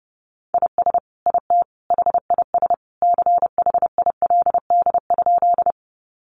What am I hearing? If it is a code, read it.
Morse code, heard as a keyed tone, about 30 wpm, 700 Hz. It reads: IS IT HIS CHILD?
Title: Sentences from Top 800 Words 30wpm